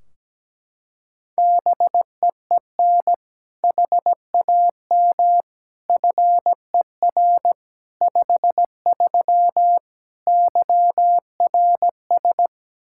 Morse code, heard as a keyed tone, about 17 wpm, 700 Hz. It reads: BEEN HAM FER 53 YRS
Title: Practice QSO - Exchange 3 - More Info - 17wpm